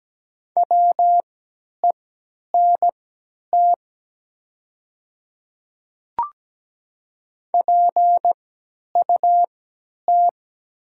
Morse code, heard as a keyed tone, about 17 wpm, spaced as for 10 wpm, 700 Hz. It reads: WENT PUT
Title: 3x Character Spacing - Top 200 Words - 17wpm